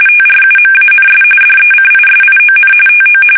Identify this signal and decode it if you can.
RTTY, about 45.45 baud. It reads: TRUMANS DECISION TO
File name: fsk850